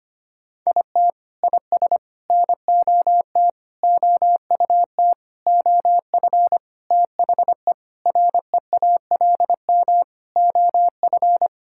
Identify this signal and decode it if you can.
Morse code, heard as a keyed tone, about 25 wpm, 700 Hz. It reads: IT IS NOT OUT OF THE REALM OF
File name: practice / 25wpm / Sentences with 9-Word Phrases - 25wpm